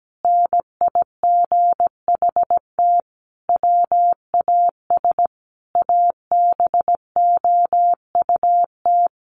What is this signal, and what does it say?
Morse code, heard as a keyed tone, about 17 wpm, 700 Hz. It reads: NIGHT WAS ABOUT
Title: Sentences from Top 1500 Words 17wpm